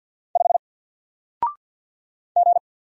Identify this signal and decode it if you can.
Morse code, heard as a keyed tone, about 48 wpm, 700 Hz. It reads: 5 G
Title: Single Letter-Number Alt 50wpm